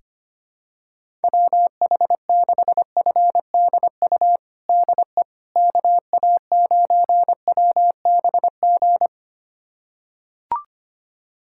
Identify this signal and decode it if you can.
Morse code, heard as a keyed tone, about 25 wpm, 700 Hz. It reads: WH6FDU DE KA9WBG
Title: USA Callsign DE Callsign - 6 Characters - 25wpm